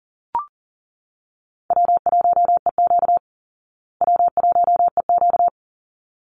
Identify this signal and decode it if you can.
Morse code, heard as a keyed tone, about 40 wpm, 700 Hz. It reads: W1EQ W1EQ
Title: Easy Call Signs Repeated 2x - 40wpm